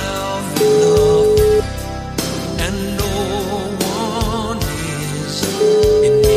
Ringback tone (Europe, 425 Hz), with noise.